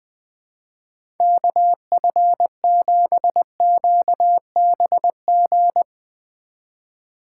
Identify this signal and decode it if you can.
Morse code, heard as a keyed tone, about 20 wpm, 700 Hz. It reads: KF7QBG
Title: USA Call Signs - 6 Characters - 2x Character Spacing - 20wpm